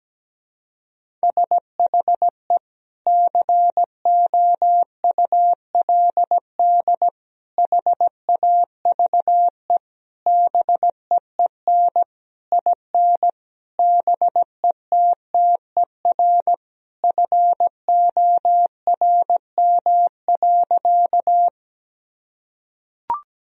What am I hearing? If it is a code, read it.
Morse code, heard as a keyed tone, about 17 wpm, 700 Hz. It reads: SHE COULD HAVE BEEN IN BETTER FORM.
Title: Sentences from Top 1000 Words - Review Part 1 - 17wpm